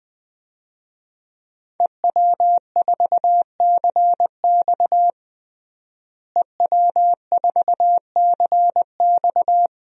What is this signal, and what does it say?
Morse code, heard as a keyed tone, about 21 wpm, 700 Hz. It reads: EW4CX EW4CX
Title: DXCC-100 Call Signs - 5 Characters - Repeated 2x - 20wpm